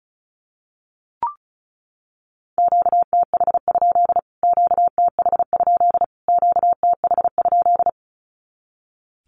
Morse code, heard as a keyed tone, about 34 wpm, 700 Hz. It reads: QTH? QTH? QTH?
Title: Q-Codes Repeated 3x 35wpm